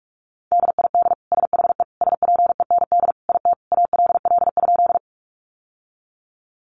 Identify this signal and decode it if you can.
Morse code, heard as a keyed tone, about 44 wpm, 700 Hz. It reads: DID SHE SPEND IT ALL?
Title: Sentences from Top 800 Words 45wpm